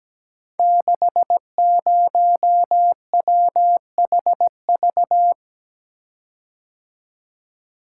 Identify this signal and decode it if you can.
Morse code, heard as a keyed tone, about 17 wpm, 700 Hz. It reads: 60WHV
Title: Code Groups - 5 Letter:Numbers - 17wpm